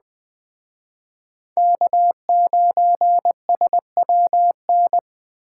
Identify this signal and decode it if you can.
Morse code, heard as a keyed tone, about 20 wpm, 700 Hz. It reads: K9SWN
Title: Speed Racing - USA 1x3 Call Signs 17wpm